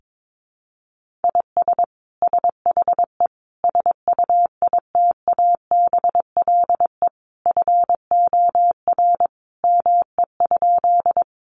Morse code, heard as a keyed tone, about 22 wpm, 700 Hz. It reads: IS SHE SUITABLE FOR ME?
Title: Sentences from Top 1400 Words 22wpm